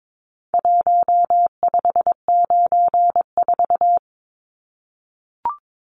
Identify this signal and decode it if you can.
Morse code, heard as a keyed tone, about 22 wpm, 700 Hz. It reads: 1594
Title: Numbers - Contest Style - 4-Digit 22wpm